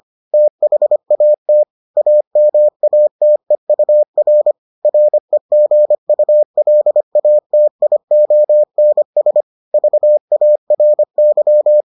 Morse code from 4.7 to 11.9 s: REGULATIONS VARY

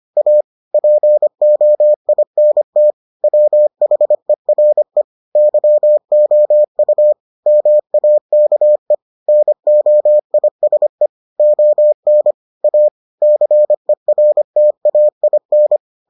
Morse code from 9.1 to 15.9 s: NOISE ON A CERTAIN

